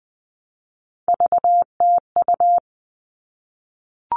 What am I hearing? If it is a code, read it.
Morse code, heard as a keyed tone, about 20 wpm, 700 Hz. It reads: VTU
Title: All Cut Numbers - 3-Digit 20wpm